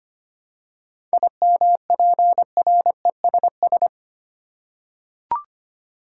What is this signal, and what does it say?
Morse code, heard as a keyed tone, about 25 wpm, 700 Hz. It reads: IMPRESS
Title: Seven Letter Words - 25wpm